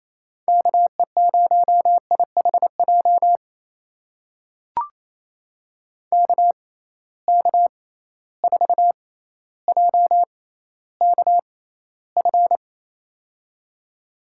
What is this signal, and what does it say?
Morse code, heard as a keyed tone, about 28 wpm, spaced as for 14 wpm, 700 Hz. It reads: KE0IHJ KK4JKF
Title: USA Call Signs - 6 Characters - 6x Character Spacing - 28wpm